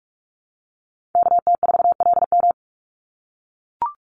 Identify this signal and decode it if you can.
Morse code, heard as a keyed tone, about 44 wpm, 700 Hz. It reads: KT4LM
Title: USA Call Signs 45wpm